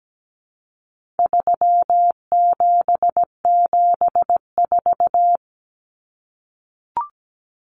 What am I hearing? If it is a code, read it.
Morse code, heard as a keyed tone, about 17 wpm, 700 Hz. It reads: 3774
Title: Common Cut Numbers - 4-Digit 17wpm